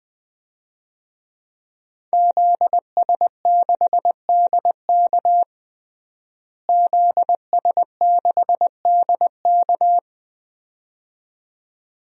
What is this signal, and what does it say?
Morse code, heard as a keyed tone, about 20 wpm, 700 Hz. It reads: ZS6DK ZS6DK
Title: DXCC-100 Call Signs - 5 Characters - Repeated 2x - 20wpm